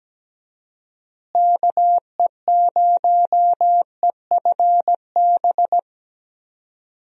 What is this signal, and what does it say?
Morse code, heard as a keyed tone, about 17 wpm, 700 Hz. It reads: KE0EFB
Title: USA Call Signs - 6 Characters - Repeated 3x - 17wpm